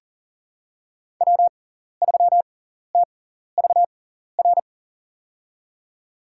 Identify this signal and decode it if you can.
Morse code, heard as a keyed tone, about 40 wpm, spaced as for 14 wpm, 700 Hz. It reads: W3TVR
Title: USA Call Signs - 5 Characters - 6x Character Spacing - 40wpm